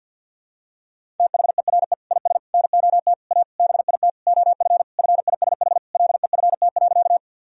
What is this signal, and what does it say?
Morse code, heard as a keyed tone, about 52 wpm, 700 Hz. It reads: THERE IS NOT A BIT OF FISH LEFT.
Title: Sentences from Top 1000 Words - Review Part 1 - 50wpm